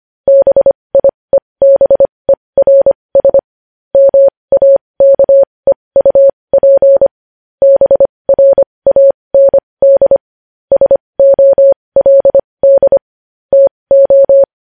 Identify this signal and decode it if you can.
Morse code, heard as a keyed tone, about 25 wpm, spaced as for 20 wpm, 550 Hz. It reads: BIEBERS MAKEUP BRAND SOLD TO
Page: news-headlines-in-morse-code-at-20-wpm